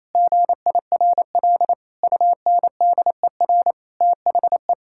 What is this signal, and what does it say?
Morse code, heard as a keyed tone, about 28 wpm, 700 Hz. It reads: GIRL UNDER THE
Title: Sentences from Top 1600 Words 28wpm